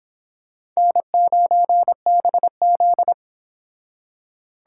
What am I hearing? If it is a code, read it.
Morse code, heard as a keyed tone, about 26 wpm, 700 Hz. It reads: N9BZ